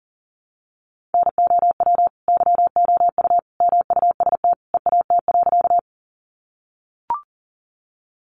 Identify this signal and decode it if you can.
Morse code, heard as a keyed tone, about 40 wpm, 700 Hz. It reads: NOW YOU MUST EAT.
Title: Sentences from Top 300 Words 40wpm